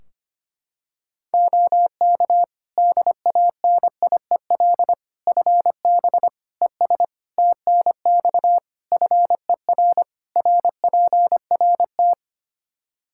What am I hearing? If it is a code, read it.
Morse code, heard as a keyed tone, about 25 wpm, 700 Hz. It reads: OK DANIEL FB ES TNX FER RPRT
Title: Practice QSO - Exchange 2 - Info - 25wpm